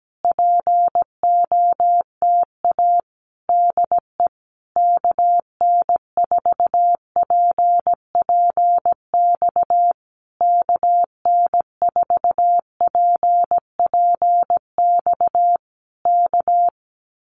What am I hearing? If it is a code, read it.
Morse code, heard as a keyed tone, about 17 wpm, 700 Hz. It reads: POTA DE KN4PPX KN4PPX K
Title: Common QSO Phrases Encore 17wpm